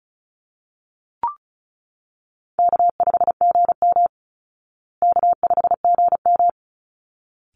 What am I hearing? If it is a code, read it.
Morse code, heard as a keyed tone, about 34 wpm, 700 Hz. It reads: K5GM K5GM
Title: Easy Call Signs Repeated 2x - 35wpm